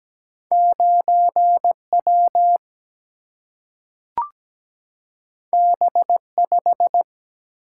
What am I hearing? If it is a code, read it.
Morse code, heard as a keyed tone, about 17 wpm, 700 Hz. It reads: 9W B5
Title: Single-Letter Single-Number 17wpm